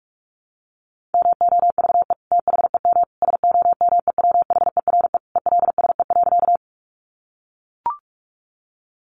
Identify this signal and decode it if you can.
Morse code, heard as a keyed tone, about 44 wpm, 700 Hz. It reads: MOVE THEM SOMEWHERE ELSE.
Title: Sentences from Top 1400 Words 45wpm